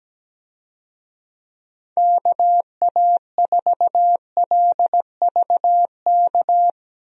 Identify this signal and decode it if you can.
Morse code, heard as a keyed tone, about 17 wpm, 700 Hz. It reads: KA4LVK
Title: USA Call Signs - 6 Characters - Repeated 3x - 17wpm